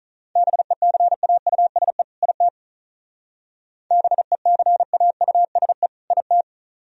Morse code, heard as a keyed tone, about 43 wpm, 700 Hz. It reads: BECAUSE IT BECAUSE IT
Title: Speed-Racing - Sets of 2 Words 35wpm